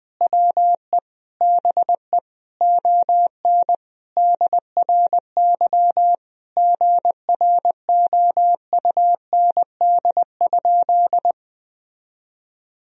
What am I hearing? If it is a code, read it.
Morse code, heard as a keyed tone, about 20 wpm, 700 Hz. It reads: WE BE ON DRY GROUND?
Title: Sentences from Top 1000 Words - Review Part 3 - 20wpm